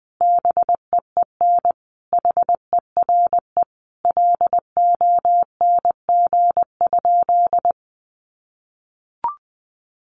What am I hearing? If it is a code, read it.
Morse code, heard as a keyed tone, about 20 wpm, 700 Hz. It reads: BEEN HERE LONG?
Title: Sentences from Top 200 Words 20wpm